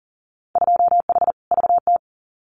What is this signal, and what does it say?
Morse code, heard as a keyed tone, about 37 wpm, 700 Hz. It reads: 2H VT
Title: Winter Field Day Exchanges 40wpm